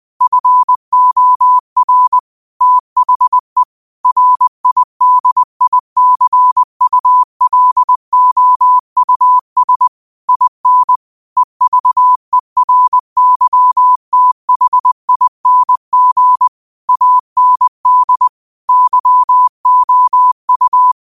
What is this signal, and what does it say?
Morse code, heard as a keyed tone, about 20 wpm, 1000 Hz. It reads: FOR THE RIDICULOUS IN EVERYTHING AND YOU